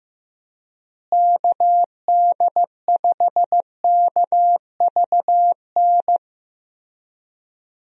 Morse code, heard as a keyed tone, about 15 wpm, 700 Hz. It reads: KD5KVN
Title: USA Call Signs - 6 Characters - Repeated 3x - 15wpm